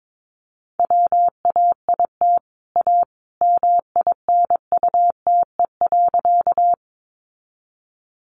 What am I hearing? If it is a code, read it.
Morse code, heard as a keyed tone, about 22 wpm, 700 Hz. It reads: WAIT A MINUTE.
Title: Sentences from Top 500 Words 22wpm